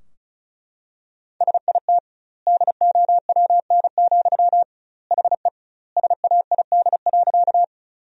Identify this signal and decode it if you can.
Morse code, heard as a keyed tone, about 34 wpm, 700 Hz. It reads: SIT DOWN, HE SAID.
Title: 2x Word-Spacing - Sentences from Top 500 Words - 35wpm